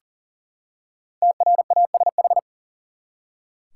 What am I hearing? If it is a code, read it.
Morse code, heard as a keyed tone, about 40 wpm, 700 Hz. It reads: TRASH
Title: Five Letter Words - 40wpm